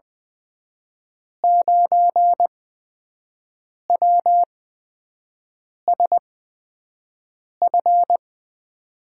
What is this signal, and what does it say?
Morse code, heard as a keyed tone, about 20 wpm, spaced as for 5 wpm, 700 Hz. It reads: 9WSF
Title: USA Call Signs - 5 Characters - 8x Character Spacing - 20wpm